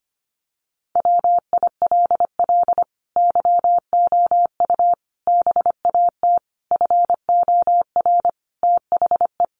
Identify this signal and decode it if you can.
Morse code, heard as a keyed tone, about 25 wpm, 700 Hz. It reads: WILL YOU BAT FOR THE
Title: Sentences from Top 1500 Words 25wpm